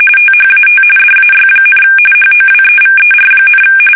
RTTY, about 45.45 baud. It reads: THAT DAY, HOWEVER,
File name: fsk850 / battleshipnj